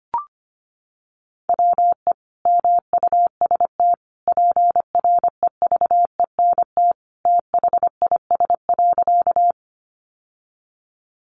Morse code, heard as a keyed tone, about 25 wpm, 700 Hz. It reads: WE MUST PREVENT THIS.
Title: Sentences from Top 1100 Words 25wpm